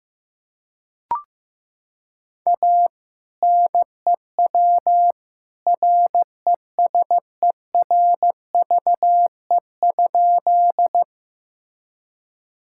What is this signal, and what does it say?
Morse code, heard as a keyed tone, about 15 wpm, 700 Hz. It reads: A NEW RESERVE?
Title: Sentences from Top 1000 Words - Review Part 6 - 15wpm